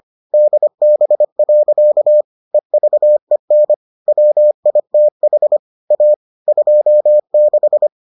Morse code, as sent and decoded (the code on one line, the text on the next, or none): -.. -... .-.-.- / . ...- . -. / .-- .. - .... / .- / ..--- -....
DB. EVEN WITH A 26